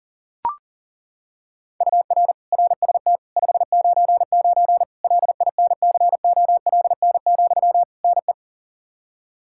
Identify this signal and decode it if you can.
Morse code, heard as a keyed tone, about 40 wpm, 700 Hz. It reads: UR RST 599 LINCOLN, NE
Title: Practice QSO - Exchange 1 - Beginning - 40wpm